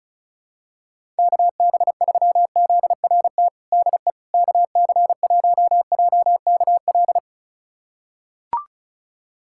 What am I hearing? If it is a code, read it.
Morse code, heard as a keyed tone, about 36 wpm, 700 Hz. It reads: KB3ZRT DE KC1JKL
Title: USA Callsign DE Callsign 35wpm